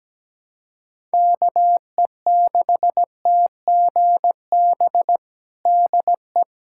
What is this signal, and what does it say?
Morse code, heard as a keyed tone, about 17 wpm, 700 Hz. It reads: KE6TGB DE
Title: USA Callsign DE Callsign 17wpm